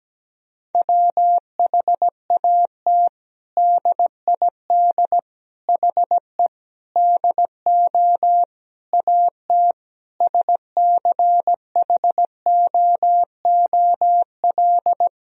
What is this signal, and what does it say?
Morse code, heard as a keyed tone, about 17 wpm, 700 Hz. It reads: WHAT DID HE DO AT SCHOOL
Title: Sentences from Top 1000 Words - Review Part 1 - 17wpm